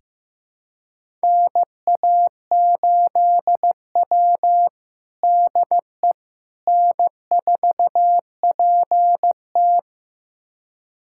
Morse code, heard as a keyed tone, about 15 wpm, 700 Hz. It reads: NA8W DE N4PT
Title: USA Callsign DE Callsign - 4 Characters - 15wpm